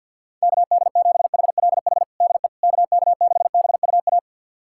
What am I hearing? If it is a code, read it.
Morse code, heard as a keyed tone, about 53 wpm, 700 Hz. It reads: KD7HLH DE KK6BUA
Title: USA Callsign DE Callsign - 6 Characters - 50wpm